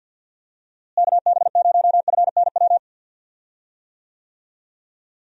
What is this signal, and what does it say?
Morse code, heard as a keyed tone, about 51 wpm, 700 Hz. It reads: KB0FNW
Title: USA Call Signs - 6 Characters - Repeated 2x - 50wpm